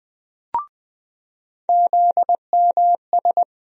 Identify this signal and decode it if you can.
Morse code, heard as a keyed tone, about 20 wpm, 700 Hz. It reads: ZMS
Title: USA Call Sign Postfixes - 3 Letters 20wpm